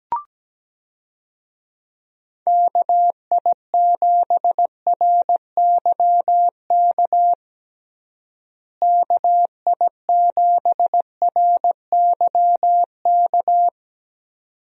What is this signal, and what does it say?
Morse code, heard as a keyed tone, about 17 wpm, 700 Hz. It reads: KI7RYK KI7RYK
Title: USA Call Signs - 6 Characters - Repeated 2x - 17wpm